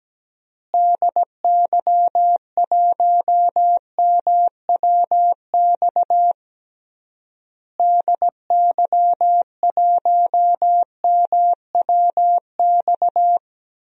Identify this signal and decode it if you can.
Morse code, heard as a keyed tone, about 17 wpm, 700 Hz. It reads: DY1MWX DY1MWX
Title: DXCC-100 Call Signs - 6 Characters - Repeated 3x - 17wpm